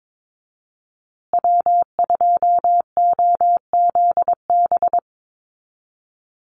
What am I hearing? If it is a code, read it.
Morse code, heard as a keyed tone, about 22 wpm, 700 Hz. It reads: W2OZB
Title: Call Signs Repeated 2x - 22wpm